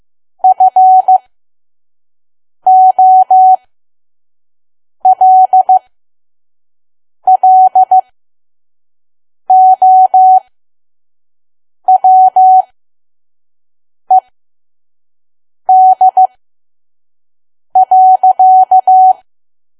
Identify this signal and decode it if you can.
Morse code, heard as a keyed tone, about 15 wpm, spaced as for 5 wpm, 750 Hz. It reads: FOLLOWED.
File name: Archive / 5 WPM files / PM